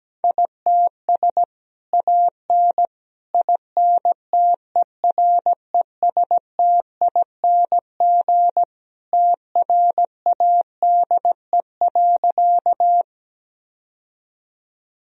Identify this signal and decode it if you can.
Morse code, heard as a keyed tone, about 17 wpm, 700 Hz. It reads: ITS AN INTERESTING TRADE.
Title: Sentences from Top 1000 Words - Review Part 4 - 17wpm